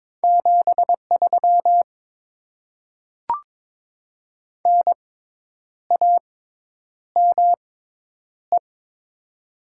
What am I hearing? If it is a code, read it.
Morse code, heard as a keyed tone, about 23 wpm, spaced as for 9 wpm, 690 Hz. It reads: 73 NAME
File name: Top 20 QSO Elements - 6x Character Spacing - 22wpm